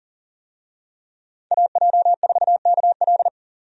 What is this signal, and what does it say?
Morse code, heard as a keyed tone, about 40 wpm, 700 Hz. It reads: AJ4KL
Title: USA Call Signs - 5 Characters - 40wpm